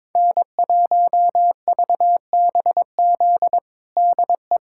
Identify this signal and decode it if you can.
Morse code, heard as a keyed tone, about 22 wpm, 700 Hz. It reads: N1VBZ DE
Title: USA Callsign DE Callsign - 5 Characters - 22wpm